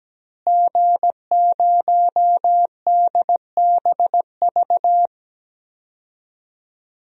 Morse code, heard as a keyed tone, about 17 wpm, 700 Hz. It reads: G0DBV
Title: DXCC-100 Call Signs - 5 Characters - Repeated 2x - 17wpm